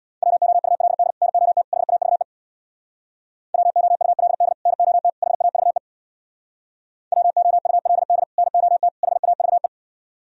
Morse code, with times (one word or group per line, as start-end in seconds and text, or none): 0.2-1.1 WOULD
1.2-1.6 NOT
1.7-2.2 HAVE
3.5-4.5 WOULD
4.7-5.1 NOT
5.2-5.8 HAVE
7.1-8.2 WOULD
8.4-8.9 NOT
9.0-9.7 HAVE